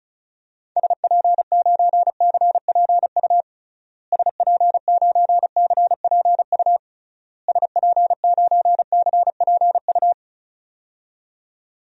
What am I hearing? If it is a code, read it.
Morse code, heard as a keyed tone, about 36 wpm, 700 Hz. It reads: SP9CPU SP9CPU SP9CPU
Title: DXCC-100 Call Signs - 6 Characters - Repeated 3x - 35wpm